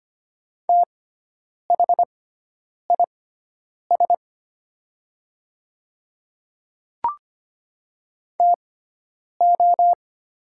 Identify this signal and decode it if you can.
Morse code, heard as a keyed tone, about 25 wpm, spaced as for 9 wpm, 700 Hz. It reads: THIS TO
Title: 6x Character Spacing - Top 100 Words - 25wpm